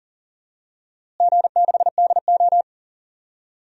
Morse code, heard as a keyed tone, about 39 wpm, 700 Hz. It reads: G6DO
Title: DXCC-100 Call Signs - 4 Characters - Repeated 2x - 40wpm